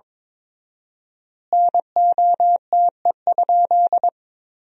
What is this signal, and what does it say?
Morse code, heard as a keyed tone, about 22 wpm, 700 Hz. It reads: NOTE?
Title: 4x Word-Spacing - Sentences from Top 500 Words - 22wpm